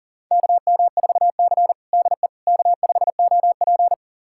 Morse code, heard as a keyed tone, about 40 wpm, 700 Hz. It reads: KM4C DE K5OP
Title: USA Callsign DE Callsign - 4 Characters - 40wpm